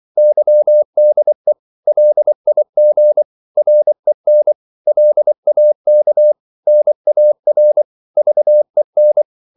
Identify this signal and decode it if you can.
Morse code, heard as a keyed tone, about 24 wpm, 600 Hz. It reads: YDE LIG REN LAK NAR VEN
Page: Korte danske ord 24wpm